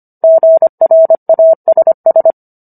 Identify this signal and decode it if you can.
Morse code, heard as a keyed tone, about 25 wpm, 650 Hz. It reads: GRASS